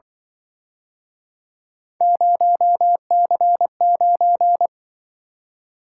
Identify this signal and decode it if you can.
Morse code, heard as a keyed tone, about 24 wpm, 700 Hz. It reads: 0C9
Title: Speed-Racing - Three Letter-Number Combinations - 20wpm